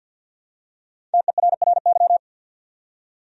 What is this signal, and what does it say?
Morse code, heard as a keyed tone, about 48 wpm, 700 Hz. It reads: TERRY
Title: USA Common Names - 50wpm